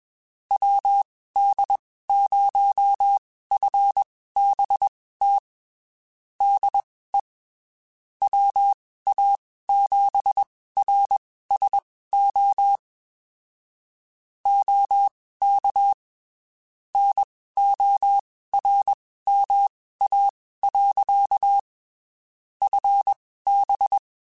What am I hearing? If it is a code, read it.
Morse code, heard as a keyed tone, about 21 wpm, spaced as for 15 wpm, 800 Hz. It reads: WD0FBT DE WA7RSO OK NORMA. FB